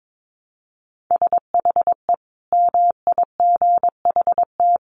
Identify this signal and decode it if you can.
Morse code, heard as a keyed tone, about 22 wpm, 700 Hz. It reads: SHE MIGHT
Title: Sentences from Top 1000 Words 22wpm